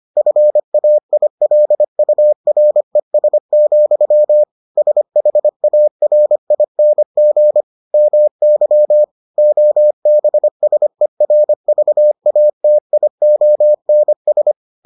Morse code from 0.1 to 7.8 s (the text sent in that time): FAILURES, SHARING